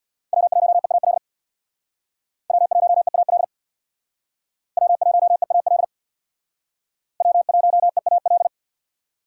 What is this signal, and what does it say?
Morse code, heard as a keyed tone, about 65 wpm, 700 Hz. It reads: W1EAL W1EAL W1EAL W1EAL